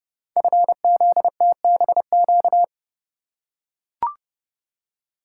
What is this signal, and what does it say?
Morse code, heard as a keyed tone, about 30 wpm, 700 Hz. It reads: FZTBQ
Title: Code Groups - 5 Letters - 30wpm